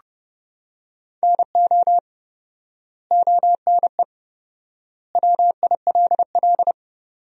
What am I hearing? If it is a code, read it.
Morse code, heard as a keyed tone, about 30 wpm, 700 Hz. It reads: NO ONE WILL
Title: 4x Word-Spacing - Sentences from Top 100 Words - 30wpm